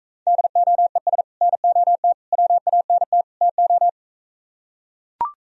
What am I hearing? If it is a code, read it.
Morse code, heard as a keyed tone, about 45 wpm, 700 Hz. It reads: DOES NOT WANT TO